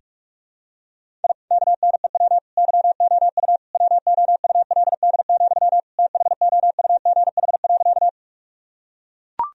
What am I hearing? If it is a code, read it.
Morse code, heard as a keyed tone, about 42 wpm, 700 Hz. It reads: I KNEW YOU WOULD, THOUGH.
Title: Sentences from Top 1000 Words - Review Part 2 - 45wpm